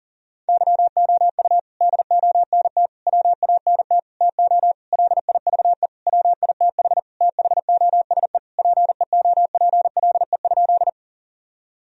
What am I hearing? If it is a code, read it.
Morse code, heard as a keyed tone, about 40 wpm, 700 Hz. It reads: YOU DONT WANT TO LIVE WITH THOSE PEOPLE?
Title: Sentences from Top 1000 Words - Review Part 1 - 40wpm